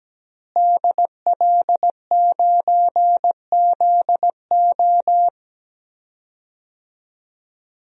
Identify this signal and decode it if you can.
Morse code, heard as a keyed tone, about 17 wpm, 700 Hz. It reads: DL9ZO
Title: DXCC-100 Call Signs - 5 Characters - Repeated 2x - 17wpm